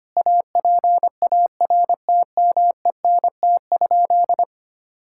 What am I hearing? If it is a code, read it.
Morse code, heard as a keyed tone, about 25 wpm, 700 Hz. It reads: APARTMENT?
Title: Sentences from Top 1000 Words - Review Part 6 - 25wpm